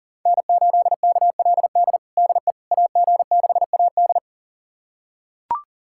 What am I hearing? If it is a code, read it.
Morse code, heard as a keyed tone, about 40 wpm, 700 Hz. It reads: N8KLD DE AG6AD
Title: USA Callsign DE Callsign - 5 Characters - 40wpm